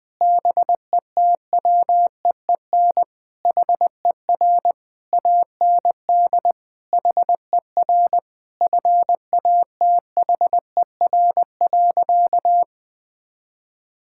Morse code, heard as a keyed tone, about 20 wpm, 700 Hz. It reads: BETWEEN HER AND HER FATHER.
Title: Sentences from Top 1000 Words - Review Part 6 - 20wpm